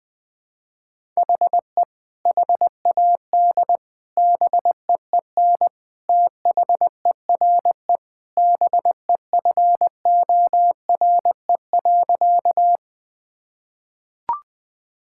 Morse code, heard as a keyed tone, about 21 wpm, 700 Hz. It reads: HE HAD BEEN THERE BEFORE.
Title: Sentences with 3-Word Phrases - 20wpm